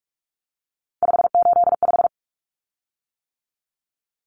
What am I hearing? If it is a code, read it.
Morse code, heard as a keyed tone, about 44 wpm, 700 Hz. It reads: 585
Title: Common Cut Numbers - 3-Digit 45wpm